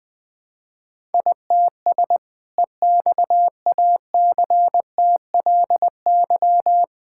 Morse code, heard as a keyed tone, about 21 wpm, 700 Hz. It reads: ITS EXACTLY
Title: Sentences from Top 1000 Words - Review Part 6 - 20wpm